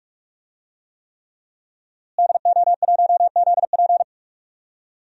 Morse code, heard as a keyed tone, about 46 wpm, 700 Hz. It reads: DO1ZP
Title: DXCC-100 Call Signs - 5 Characters - Repeated 3x - 45wpm